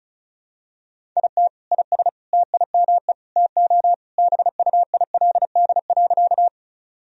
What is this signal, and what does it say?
Morse code, heard as a keyed tone, about 34 wpm, 700 Hz. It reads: IT IS TIME TO BUILD.
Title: Sentences from Top 1000 Words - Review Part 1 - 35wpm